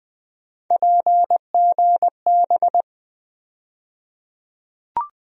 Morse code, heard as a keyed tone, about 20 wpm, 700 Hz. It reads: PGB
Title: USA Call Sign Postfixes - 3 Letters 20wpm